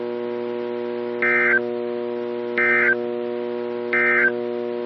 Emergency Alert System (SAME) tones, heard as harsh buzzing bursts. It is heard from 1.2 to 4.4 s.
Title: NOAA Weather Radio - KEC55